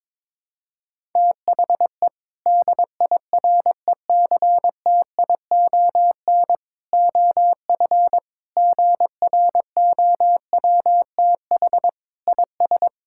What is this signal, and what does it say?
Morse code, heard as a keyed tone, about 22 wpm, 700 Hz. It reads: THE DIRECTION OF GROWTH IS
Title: Sentences from Top 1000 Words - Review Part 5 - 22wpm